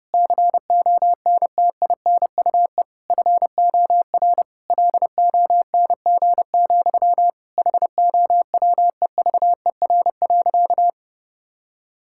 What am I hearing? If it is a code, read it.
Morse code, heard as a keyed tone, about 32 wpm, 700 Hz. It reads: CONTINUE FOR LONG, HOWEVER.
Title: Sentences from Top 1000 Words - Review Part 6 - 30wpm